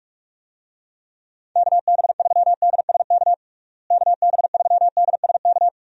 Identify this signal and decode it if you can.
Morse code, heard as a keyed tone, about 46 wpm, 700 Hz. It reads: KB3DSK KB3DSK
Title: USA Call Signs - 6 Characters - Repeated 2x - 45wpm